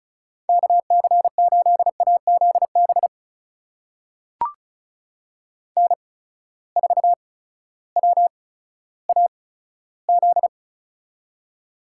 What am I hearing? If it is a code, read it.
Morse code, heard as a keyed tone, about 36 wpm, spaced as for 16 wpm, 700 Hz. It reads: KC8AZB N4WAZ
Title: USA Call Signs - 8x Character Spacing - 35wpm